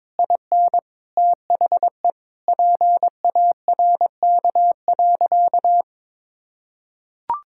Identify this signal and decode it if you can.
Morse code, heard as a keyed tone, about 22 wpm, 700 Hz. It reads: IN THE PARK.
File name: practice / 22wpm / Sentences with 3-Word Phrases - 22wpm